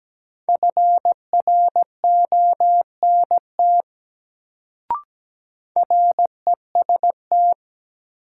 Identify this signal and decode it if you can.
Morse code, heard as a keyed tone, about 17 wpm, 700 Hz. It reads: FRONT REST
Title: Sentences from Top 500 Words - New Word Drill - 17wpm